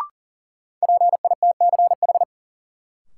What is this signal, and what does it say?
Morse code, heard as a keyed tone, about 40 wpm, 700 Hz. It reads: PITCH
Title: Sentences from Top 1300 Words - New Word Drill - 40wpm